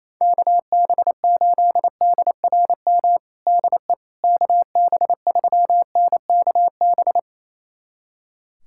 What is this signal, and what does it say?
Morse code, heard as a keyed tone, about 28 wpm, 700 Hz. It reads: KB8DRM DE KB3NKB
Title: USA Callsign DE Callsign - 6 Characters - 28wpm